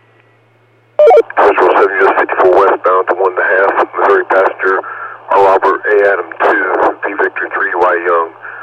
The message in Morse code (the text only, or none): none